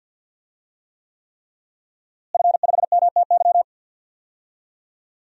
Morse code, heard as a keyed tone, about 51 wpm, 700 Hz. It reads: F5MTY
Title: DXCC-100 Call Signs - 5 Characters - Repeated 3x - 50wpm